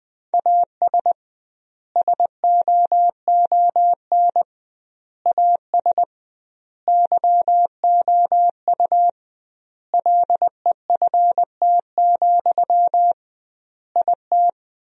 Morse code, heard as a keyed tone, about 20 wpm, 700 Hz. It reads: AS SOON AS YOU LEFT, IT
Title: 2x Word-Spacing - Sentences from Top 400 Words - 20wpm